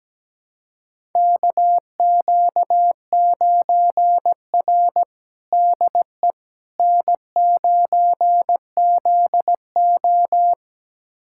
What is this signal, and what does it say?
Morse code, heard as a keyed tone, about 17 wpm, 700 Hz. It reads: KQ9R DE N9ZO
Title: USA Callsign DE Callsign - 4 Characters - 17wpm